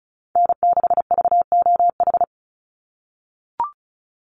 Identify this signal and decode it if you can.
Morse code, heard as a keyed tone, about 34 wpm, 700 Hz. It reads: N6VOH
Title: Call Signs - 35wpm